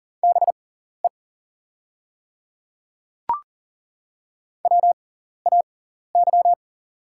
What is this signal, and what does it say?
Morse code, heard as a keyed tone, about 40 wpm, spaced as for 14 wpm, 700 Hz. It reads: BE WAY
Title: 6x Character Spacing - Top 200 Words - 40wpm